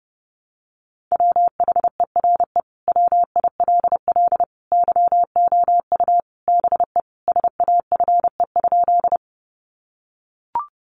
Morse code, heard as a keyed tone, about 30 wpm, 700 Hz. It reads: WHERE WILL YOU BE SAFE?
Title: Sentences from Top 900 Words 30wpm